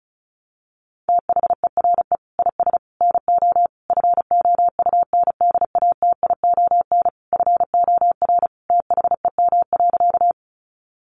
Morse code, heard as a keyed tone, about 34 wpm, 700 Hz. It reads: THERE IS NO FOUNDATION FOR THEM.
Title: Sentences from Top 1200 Words 35wpm